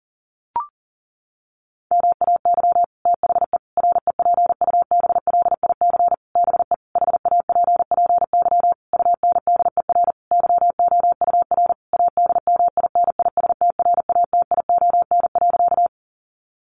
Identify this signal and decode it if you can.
Morse code, heard as a keyed tone, about 37 wpm, 700 Hz. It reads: MAY THE REPUBLIC BE HAPPY UNDER YOUR ADMINISTRATION.
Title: Sentences from Top 1500 Words 40wpm